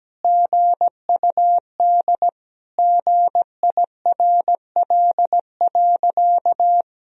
Morse code, heard as a keyed tone, about 17 wpm, 700 Hz. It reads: GUD GIRL.
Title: Sentences from Top 400 Words and Abbreviations 17wpm